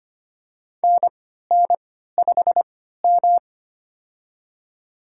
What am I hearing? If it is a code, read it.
Morse code, heard as a keyed tone, about 25 wpm, spaced as for 14 wpm, 700 Hz. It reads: NN5M
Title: USA Call Signs - 4 Characters - 3x Character Spacing - 25wpm